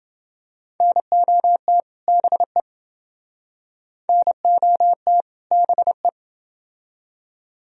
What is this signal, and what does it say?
Morse code, heard as a keyed tone, about 29 wpm, 700 Hz. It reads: NOT BE NOT BE